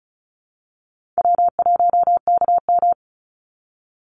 Morse code, heard as a keyed tone, about 34 wpm, 700 Hz. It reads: W1KM
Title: Call Signs Repeated 3x - 35wpm